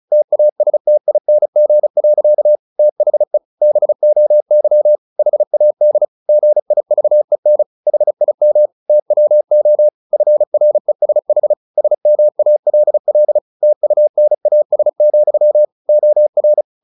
Morse code, heard as a keyed tone, about 34 wpm, 600 Hz. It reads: TASTING. THE BOY HAD GIVEN HIM TWO FRESH SMALL TUNAS, OR
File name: the old man and the sea / the old man and the sea 1 35wpm